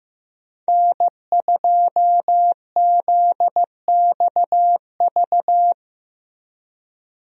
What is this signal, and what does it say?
Morse code, heard as a keyed tone, about 15 wpm, 700 Hz. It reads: N2ZXV
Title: USA Call Signs - 5 Characters - Repeated 3x - 15wpm